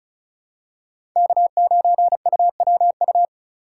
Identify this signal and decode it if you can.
Morse code, heard as a keyed tone, about 36 wpm, 700 Hz. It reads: K9UWU
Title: USA Call Signs - 5 Characters - 2x Character Spacing - 35wpm